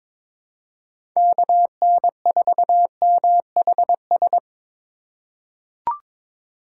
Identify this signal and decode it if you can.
Morse code, heard as a keyed tone, about 22 wpm, 700 Hz. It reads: KN4MHS
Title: USA Call Signs - 6 Characters - 2x Character Spacing - 22wpm